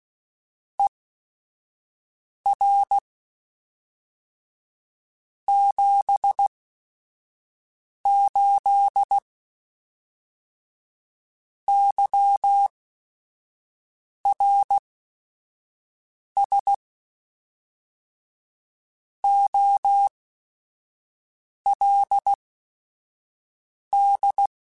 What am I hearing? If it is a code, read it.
Morse code, heard as a keyed tone, about 16 wpm, spaced as for 5 wpm, 800 Hz. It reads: ER 78 YRS OLD